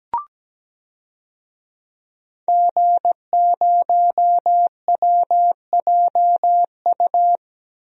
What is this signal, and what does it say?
Morse code, heard as a keyed tone, about 17 wpm, 700 Hz. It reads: G0WJU
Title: DXCC-100 Call Signs - 5 Characters - Repeated 3x - 17wpm